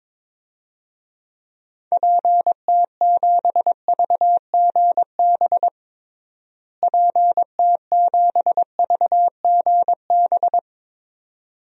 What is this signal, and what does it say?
Morse code, heard as a keyed tone, about 22 wpm, 700 Hz. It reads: PT7VGB PT7VGB
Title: DXCC-100 Call Signs - 6 Characters - Repeated 2x - 22wpm